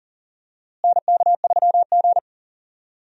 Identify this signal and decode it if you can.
Morse code, heard as a keyed tone, about 40 wpm, 700 Hz. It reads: NK3G
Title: USA Call Signs - 4 Characters - Repeated 2x - 40wpm